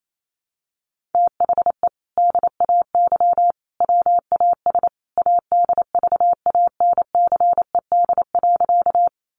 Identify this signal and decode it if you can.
Morse code, heard as a keyed tone, about 28 wpm, 700 Hz. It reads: THE DAY WAS ADVANCED.
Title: Sentences from Top 1200 Words 28wpm